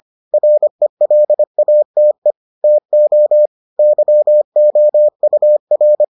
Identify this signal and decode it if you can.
Morse code, heard as a keyed tone, about 25 wpm, 600 Hz. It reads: RELATE TO YOUR